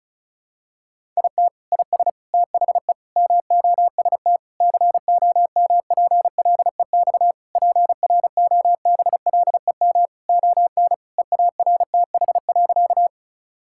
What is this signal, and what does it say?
Morse code, heard as a keyed tone, about 36 wpm, 700 Hz. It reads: IT IS THE MOST COMPLEX PROBLEM ON EARTH.
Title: Sentences from Top 1000 Words - Review Part 6 - 35wpm